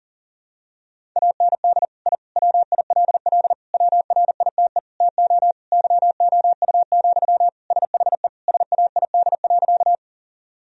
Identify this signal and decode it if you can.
Morse code, heard as a keyed tone, about 40 wpm, 700 Hz. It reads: AND I WILL WRITE TO YOU, SHE SAID.
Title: Sentences from Top 1000 Words - Review Part 1 - 40wpm